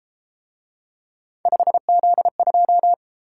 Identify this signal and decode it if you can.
Morse code, heard as a keyed tone, about 33 wpm, 700 Hz. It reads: 5Z2